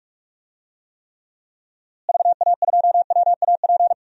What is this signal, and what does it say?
Morse code, heard as a keyed tone, about 46 wpm, 700 Hz. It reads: VA2WAP
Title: DXCC-100 Call Signs - 6 Characters - Repeated 2x - 45wpm